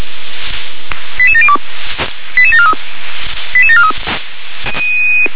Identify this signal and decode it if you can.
Five-tone selective call (ZVEI), heard as quick runs of five tones, tones of about 70 ms. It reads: 88702 88743 88743